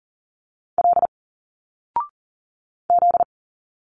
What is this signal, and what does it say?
Morse code, heard as a keyed tone, about 39 wpm, 700 Hz. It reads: L Z
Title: Single Letters 40wpm